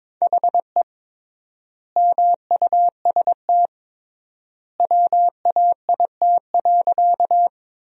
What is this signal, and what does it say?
Morse code, heard as a keyed tone, about 22 wpm, 700 Hz. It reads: HE MUST WAIT.
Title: 3x Word-Spacing - Sentences from Top 500 Words - 22wpm